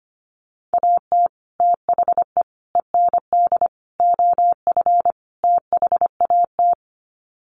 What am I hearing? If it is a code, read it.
Morse code, heard as a keyed tone, about 25 wpm, 700 Hz. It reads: AT THE END OF THAT
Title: Sets of 5 Words Encore2 25wpm